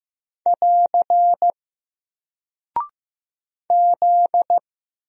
Morse code, heard as a keyed tone, about 15 wpm, 700 Hz. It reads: <AR> Z
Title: Single Character (Letter:Number:Punctuation:Prosign) 15wpm